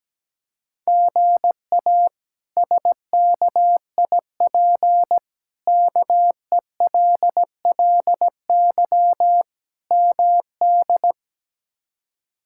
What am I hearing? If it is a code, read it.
Morse code, heard as a keyed tone, about 17 wpm, 700 Hz. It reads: GA SKIP KELLY MD
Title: K1USN SST QSO Practice - 17wpm